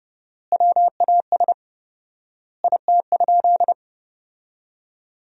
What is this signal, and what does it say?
Morse code, heard as a keyed tone, about 32 wpm, 700 Hz. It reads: WAS IT?
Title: 4x Word-Spacing - Sentences from Top 300 Words - 30wpm